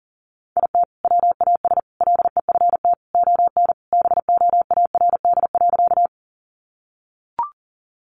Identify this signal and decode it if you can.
Morse code, heard as a keyed tone, about 40 wpm, 700 Hz. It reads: IT WAS LEFT ON BOARD.
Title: Sentences from Top 700 Words 40wpm